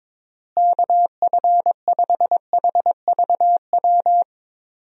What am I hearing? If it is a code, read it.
Morse code, heard as a keyed tone, about 22 wpm, 700 Hz. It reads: KF5HVW
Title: USA Call Signs - 6 Characters - Repeated 3x - 22wpm